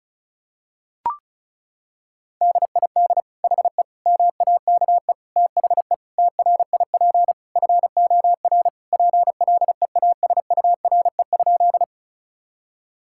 Morse code, heard as a keyed tone, about 34 wpm, 700 Hz. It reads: DID HE MAKE THE TRIP FOR PLEASURE?
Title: Sentences from Top 1600 Words 35wpm